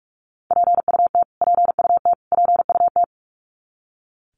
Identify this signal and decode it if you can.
Morse code, heard as a keyed tone, about 44 wpm, 700 Hz. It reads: PUT PUT PUT
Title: Top 300 Words Repeated 45wpm